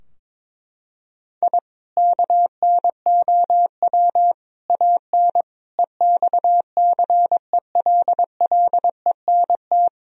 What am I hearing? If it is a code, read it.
Morse code, heard as a keyed tone, about 22 wpm, 700 Hz. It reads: I KNOW AN EXCELLENT
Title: Sentences from Top 1000 Words - Review Part 3 - 22wpm